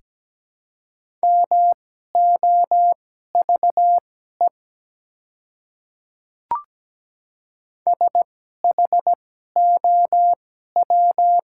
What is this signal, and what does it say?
Morse code, heard as a keyed tone, about 17 wpm, spaced as for 12 wpm, 700 Hz. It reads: MOVE SHOW
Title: 2x Character Spacing - Top 200 Words - 17wpm